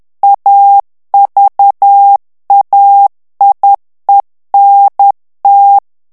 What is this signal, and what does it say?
Morse code, heard as a keyed tone, about 10 wpm, 800 Hz. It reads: AVAIENT
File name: CW / FICHIERS-MP3 / FRANCAIS TEXTE CLAIR 2 VITESSE 10 MOTS MINUTES